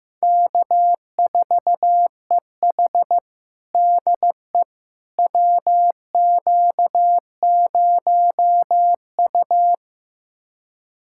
Morse code, heard as a keyed tone, about 15 wpm, 700 Hz. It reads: K4EH DE WQ0U
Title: USA Callsign DE Callsign - 4 Characters - 15wpm